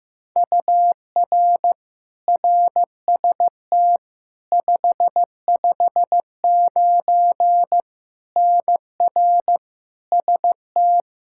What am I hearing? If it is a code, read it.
Morse code, heard as a keyed tone, about 15 wpm, 700 Hz. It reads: UR RST 559 NR ST
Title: Practice QSO - Exchange 1 - Beginning - 15wpm